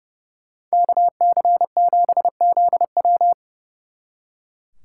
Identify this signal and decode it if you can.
Morse code, heard as a keyed tone, about 36 wpm, 700 Hz. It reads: KC7ZW
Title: Speed Racing - USA 2x2 Call Signs 30wpm